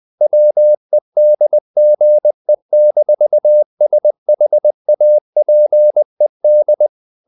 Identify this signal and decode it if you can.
Morse code, heard as a keyed tone, about 20 wpm, 600 Hz. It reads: WEDGE-SHAPED
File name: the old man and the sea / the old man and the sea part 1 20wpm